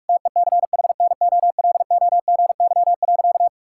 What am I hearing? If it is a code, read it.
Morse code, heard as a keyed tone, about 41 wpm, 700 Hz. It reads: TECHNOLOGY.
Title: Sentences from Top 1000 Words - Review Part 5 - 45wpm